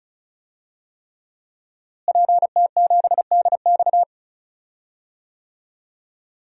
Morse code, heard as a keyed tone, about 36 wpm, 700 Hz. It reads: PT7DX
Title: DXCC-100 Call Signs - 5 Characters - Repeated 3x - 35wpm